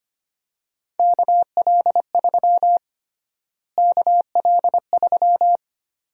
Morse code, heard as a keyed tone, about 25 wpm, 700 Hz. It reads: KL3 KL3
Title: USA Call Sign Prefixes - Repeated 2x - 25wpm